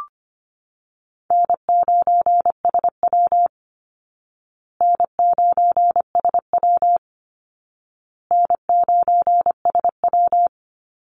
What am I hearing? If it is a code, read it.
Morse code, heard as a keyed tone, about 25 wpm, 700 Hz. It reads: N9SW N9SW N9SW
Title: Easy Call Signs Repeated 3x - 25wpm